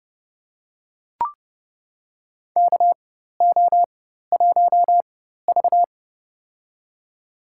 Morse code, heard as a keyed tone, about 30 wpm, spaced as for 14 wpm, 700 Hz. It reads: KO1V
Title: USA Call Signs - 4x Character Spacing - 30wpm